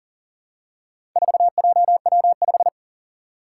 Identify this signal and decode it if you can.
Morse code, heard as a keyed tone, about 40 wpm, 700 Hz. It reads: 4JW5
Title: Four Challenging Letter-Number Combinations 40wpm